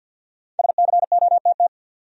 Morse code, heard as a keyed tone, about 48 wpm, 700 Hz. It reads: SCOTT
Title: USA Common Names - Max 5 Letters - 50wpm